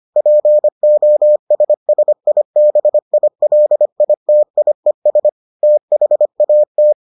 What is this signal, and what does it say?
Morse code, heard as a keyed tone, about 25 wpm, 600 Hz. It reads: POSSIBILITIES THAT